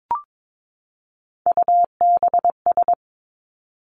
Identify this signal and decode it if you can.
Morse code, heard as a keyed tone, about 22 wpm, 700 Hz. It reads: UBS
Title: Three Challenging Letter-Number Combinations 22wpm